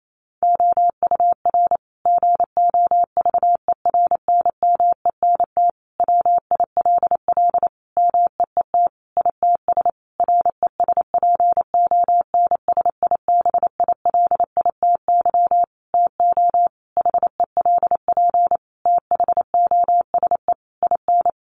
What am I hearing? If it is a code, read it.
Morse code, heard as a keyed tone, about 28 wpm, 700 Hz. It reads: OUR GOVERNMENT WILL MEET ITS RESPONSIBILITY TO HELP THOSE IN
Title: Sentences from Top 1200 Words 28wpm